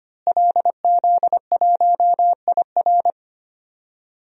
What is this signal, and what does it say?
Morse code, heard as a keyed tone, about 25 wpm, 700 Hz. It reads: LZ1IR
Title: DXCC-100 Call Signs - 5 Characters - Repeated 3x - 25wpm